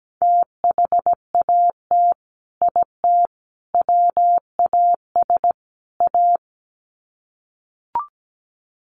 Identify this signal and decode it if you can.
Morse code, heard as a keyed tone, about 17 wpm, 700 Hz. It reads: THAT IT WAS A
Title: Sets of 4 Words 17wpm